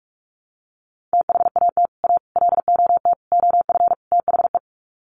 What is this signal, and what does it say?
Morse code, heard as a keyed tone, about 44 wpm, 700 Hz. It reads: THAT A LOT OF THE
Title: Sets of 5 Words Encore2 45wpm